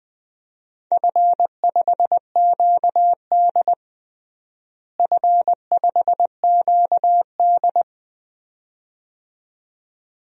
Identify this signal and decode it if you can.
Morse code, heard as a keyed tone, about 21 wpm, 700 Hz. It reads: F5QD F5QD
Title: DXCC-100 Call Signs - 4 Characters - Repeated 2x - 20wpm